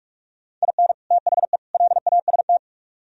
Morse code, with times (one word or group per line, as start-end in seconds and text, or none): 0.6-0.9 IN
1.1-1.6 THE
1.7-2.6 LAST